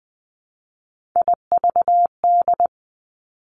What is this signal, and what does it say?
Morse code, heard as a keyed tone, about 20 wpm, 700 Hz. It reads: IVD
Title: Three Challenging Letter-Number Combinations 20wpm